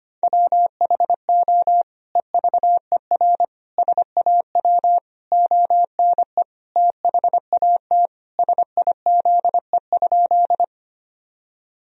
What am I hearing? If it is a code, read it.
Morse code, heard as a keyed tone, about 25 wpm, 700 Hz. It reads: WHO EVER SAW ONE THAT SIZE?
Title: Sentences from Top 1000 Words - Review Part 2 - 25wpm